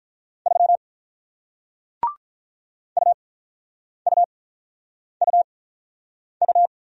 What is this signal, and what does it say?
Morse code, heard as a keyed tone, about 46 wpm, spaced as for 9 wpm, 700 Hz. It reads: 3 UUUU